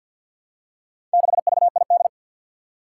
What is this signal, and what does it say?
Morse code, heard as a keyed tone, about 51 wpm, 700 Hz. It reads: 6VID
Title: Four Challenging Letter-Number Combinations 50wpm